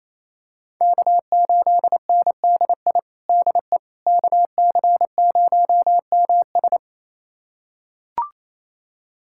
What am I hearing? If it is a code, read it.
Morse code, heard as a keyed tone, about 28 wpm, 700 Hz. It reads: K8NDI DE KC0MS
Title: USA Callsign DE Callsign - 5 Characters - 28wpm